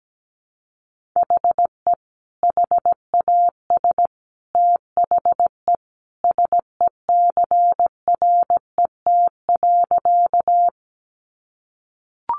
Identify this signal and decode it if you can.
Morse code, heard as a keyed tone, about 17 wpm, 700 Hz. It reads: HE HAS THE SECRET.
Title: Sentences from Top 1200 Words 17wpm